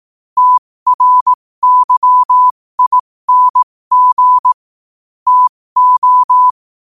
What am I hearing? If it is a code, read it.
Morse code, heard as a keyed tone, about 18 wpm, spaced as for 15 wpm, 1000 Hz. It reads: TRYING TO
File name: qotdfiles / 15wpm